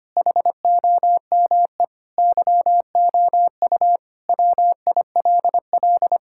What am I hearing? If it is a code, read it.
Morse code, heard as a keyed tone, about 25 wpm, 700 Hz. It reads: HOME YOU WILL
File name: practice / 25wpm / Sentences with 4-Word Phrases - 25wpm